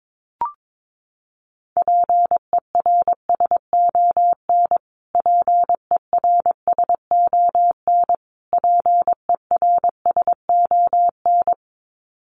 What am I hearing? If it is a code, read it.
Morse code, heard as a keyed tone, about 22 wpm, 700 Hz. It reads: PERSON PERSON PERSON
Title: Top 500 Words Repeated 22wpm